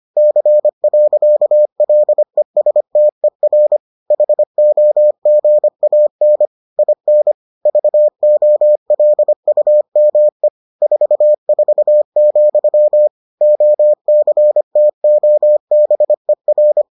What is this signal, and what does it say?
Morse code, heard as a keyed tone, about 25 wpm, 600 Hz. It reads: C. LESTER HOGAN IN VOLUME 44, OCTOBER